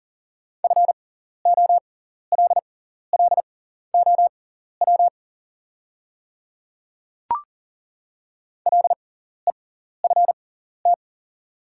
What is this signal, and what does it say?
Morse code, heard as a keyed tone, about 40 wpm, spaced as for 14 wpm, 700 Hz. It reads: FOLLOW LEFT
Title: 6x Character Spacing - Top 300 Words - 40wpm